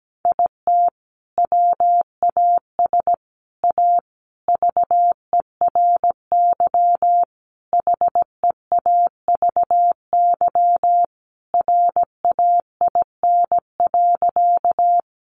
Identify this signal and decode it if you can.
Morse code, heard as a keyed tone, about 17 wpm, 700 Hz. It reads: IT WAS A VERY HEAVY RAIN.
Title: Sentences from Top 1100 Words 17wpm